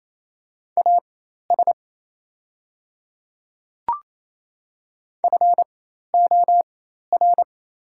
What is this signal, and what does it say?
Morse code, heard as a keyed tone, about 28 wpm, spaced as for 13 wpm, 690 Hz. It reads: AS FOR
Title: 4x Character Spacing - Top 100 Words - 28wpm